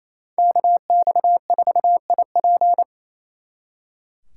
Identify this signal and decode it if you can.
Morse code, heard as a keyed tone, about 36 wpm, 700 Hz. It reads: KX4IP